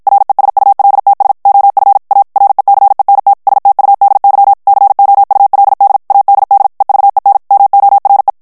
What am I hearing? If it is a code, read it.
Morse code, heard as a keyed tone, about 56 wpm, 800 Hz. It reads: RESULTS OF A RECENT STUDY COULD ADD EVEN MORE
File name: morse mp3 / ANGLAIS texte clair 1 vitesse 50 mots minute